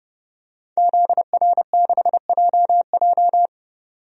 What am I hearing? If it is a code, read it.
Morse code, heard as a keyed tone, about 30 wpm, 700 Hz. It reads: ZR6JJ
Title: DXCC-100 Call Signs - 5 Characters - 30wpm